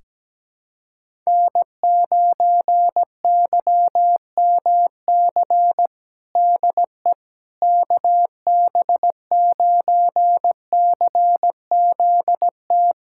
Morse code, heard as a keyed tone, about 17 wpm, 700 Hz. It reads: N9YMC DE KB9CZT
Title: USA Callsign DE Callsign 17wpm